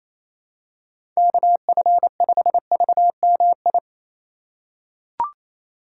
Morse code, heard as a keyed tone, about 28 wpm, 700 Hz. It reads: KF5VMI
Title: USA Call Signs - 6 Characters - 4x Character Spacing - 28wpm